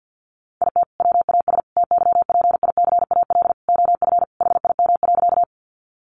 Morse code, heard as a keyed tone, about 50 wpm, 700 Hz. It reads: IT WAS TYPICAL OF HIM.
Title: Sentences from Top 1100 Words 50wpm